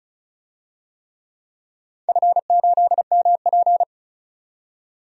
Morse code, heard as a keyed tone, about 36 wpm, 700 Hz. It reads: F8MP
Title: DXCC-100 Call Signs - 4 Characters - 35wpm